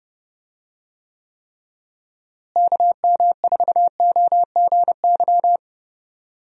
Morse code, heard as a keyed tone, about 32 wpm, 700 Hz. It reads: KM4OGY
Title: USA Call Signs - 6 Characters - Repeated 3x - 30wpm